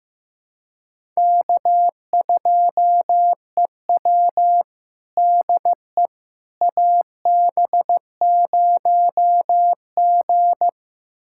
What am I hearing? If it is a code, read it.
Morse code, heard as a keyed tone, about 15 wpm, 700 Hz. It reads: K2EW DE AB0G
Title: USA Callsign DE Callsign - 4 Characters - 15wpm